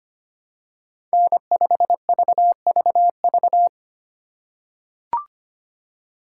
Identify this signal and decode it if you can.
Morse code, heard as a keyed tone, about 25 wpm, 700 Hz. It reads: N5VVV
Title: USA Call Signs - 5 Characters - 3x Character Spacing - 25wpm